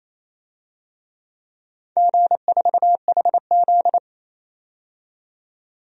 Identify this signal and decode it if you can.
Morse code, heard as a keyed tone, about 28 wpm, 700 Hz. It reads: G4HZ
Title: DXCC-100 Call Signs - 4 Characters - 28wpm